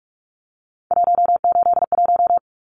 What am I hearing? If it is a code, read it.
Morse code, heard as a keyed tone, about 44 wpm, 700 Hz. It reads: 181
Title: Numbers - 3-Digit 45wpm